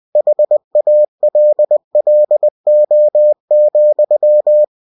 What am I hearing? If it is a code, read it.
Morse code, heard as a keyed tone, about 20 wpm, 600 Hz. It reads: HALLO,
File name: cw-books / winnie-the-pooh / winnie-20wpm-page-0106